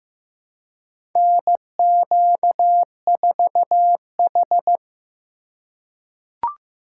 Morse code, heard as a keyed tone, about 15 wpm, 700 Hz. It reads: NQ4H
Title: USA Call Signs - 4 Characters - 3x Character Spacing - 15wpm